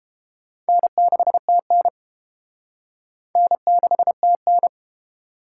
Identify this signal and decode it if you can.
Morse code, heard as a keyed tone, about 31 wpm, 700 Hz. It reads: N6TN N6TN